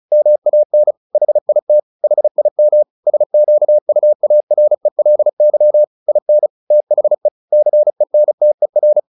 Morse code, heard as a keyed tone, about 34 wpm, 600 Hz. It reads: MAN HIT HIM SQUARELY IN THE CENTER
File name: the old man and the sea 2 35wpm